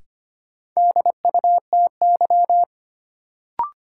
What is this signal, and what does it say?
Morse code, heard as a keyed tone, about 25 wpm, 700 Hz. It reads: DUTY
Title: Sentences from Top 1500 Words - New Word Drill Alt - 25wpm